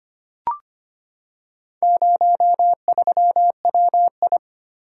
Morse code, heard as a keyed tone, about 25 wpm, 700 Hz. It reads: 03WI
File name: practice / 25wpm / Four Letter-Number Combinations - 25wpm